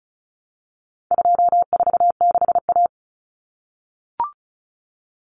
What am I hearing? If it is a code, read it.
Morse code, heard as a keyed tone, about 34 wpm, 700 Hz. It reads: 246A
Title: Common Cut Numbers - 4-Digit 35wpm